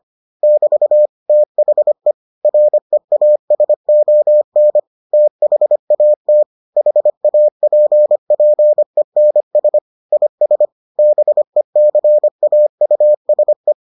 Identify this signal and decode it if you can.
Morse code, heard as a keyed tone, about 25 wpm, 600 Hz. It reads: <BT> THE REASON THAT HAPPENS IS BECAUSE